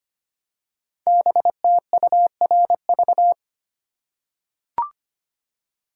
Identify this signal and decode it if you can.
Morse code, heard as a keyed tone, about 25 wpm, 700 Hz. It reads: BTURV
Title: Code Groups - 5 Letters - 25wpm